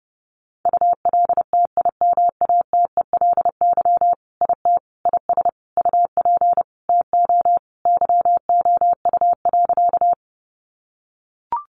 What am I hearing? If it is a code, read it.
Morse code, heard as a keyed tone, about 28 wpm, 700 Hz. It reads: ULTIMATELY IT IS UP TO YOU.
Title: Sentences from Top 1200 Words 30wpm